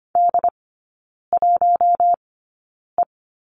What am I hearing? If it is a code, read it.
Morse code, heard as a keyed tone, about 25 wpm, spaced as for 9 wpm, 690 Hz. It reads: D1E
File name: Lesson 14e - Number 2 - Three Characters 25wpm